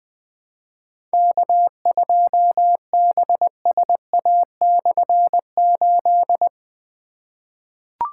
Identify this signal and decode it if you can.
Morse code, heard as a keyed tone, about 20 wpm, 700 Hz. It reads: K2BSA/8
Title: K1USN SST QSO Practice - 20wpm